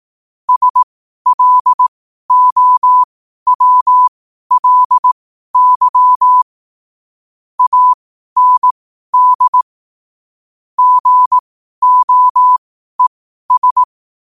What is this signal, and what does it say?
Morse code, heard as a keyed tone, about 18 wpm, spaced as for 12 wpm, 1000 Hz. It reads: SLOWLY AND GOES
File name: qotdfiles / 12wpm